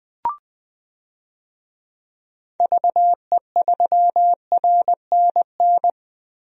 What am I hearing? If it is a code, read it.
Morse code, heard as a keyed tone, about 20 wpm, 700 Hz. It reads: VE3RNN
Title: DXCC-100 Call Signs - 6 Characters - Repeated 2x - 20wpm